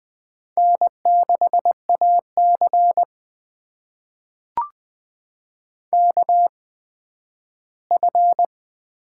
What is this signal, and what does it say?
Morse code, heard as a keyed tone, about 20 wpm, spaced as for 12 wpm, 700 Hz. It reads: N6AC KF
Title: USA Call Signs - 8x Character Spacing - 20wpm